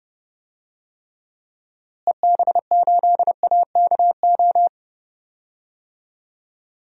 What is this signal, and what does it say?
Morse code, heard as a keyed tone, about 32 wpm, 700 Hz. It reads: EB8AKO
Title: DXCC-100 Call Signs - 6 Characters - Repeated 3x - 30wpm